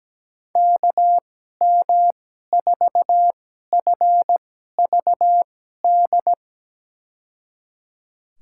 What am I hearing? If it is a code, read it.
Morse code, heard as a keyed tone, about 17 wpm, spaced as for 12 wpm, 700 Hz. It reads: KM4FVD
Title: USA Call Signs - 6 Characters - 2x Character Spacing - 17wpm